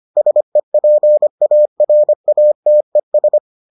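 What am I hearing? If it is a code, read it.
Morse code, heard as a keyed tone, about 25 wpm, 600 Hz. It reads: SEPARATES
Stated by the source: Morse edition of the Foundations of Amateur Radio episodes.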